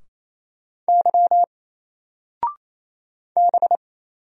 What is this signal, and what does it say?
Morse code, heard as a keyed tone, about 29 wpm, 700 Hz. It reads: Y B
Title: Single Letter-Number 28wpm